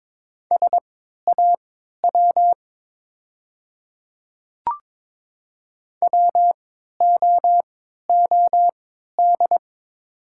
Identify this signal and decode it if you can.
Morse code, heard as a keyed tone, about 22 wpm, spaced as for 12 wpm, 700 Hz. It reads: SAW WOOD
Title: 3x Character Spacing - Top 300 Words - 22wpm